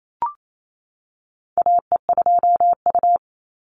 Morse code, heard as a keyed tone, about 28 wpm, 700 Hz. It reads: AE2U
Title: Call Signs - 28wpm